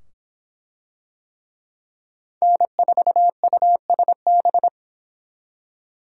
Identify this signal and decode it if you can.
Morse code, heard as a keyed tone, about 22 wpm, 700 Hz. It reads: N4USB